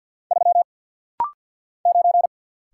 Morse code, heard as a keyed tone, about 49 wpm, 700 Hz. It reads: 3 9
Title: Single Letter-Number - Rapid-Fire 50wpm